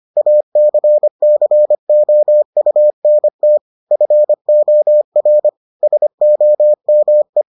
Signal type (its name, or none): Morse code